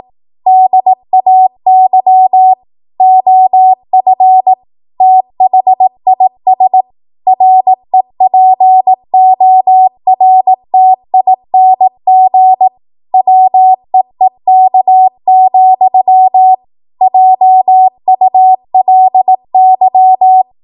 Morse code, heard as a keyed tone, about 18 wpm, 750 Hz. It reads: DAY OF THIS REPORTING WEEK, JULY
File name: Archive / 18 WPM / 18WPM